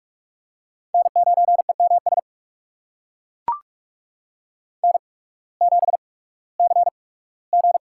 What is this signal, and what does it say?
Morse code, heard as a keyed tone, about 42 wpm, 690 Hz. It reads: N 9 E M S N 7 C G
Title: USA Call Signs - 8x Character Spacing - 45wpm